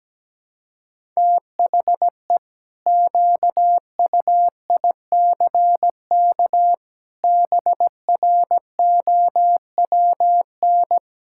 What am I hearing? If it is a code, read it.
Morse code, heard as a keyed tone, about 17 wpm, 700 Hz. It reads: THE QUICK BROWN
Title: Pangram Challenge v2 - 17wpm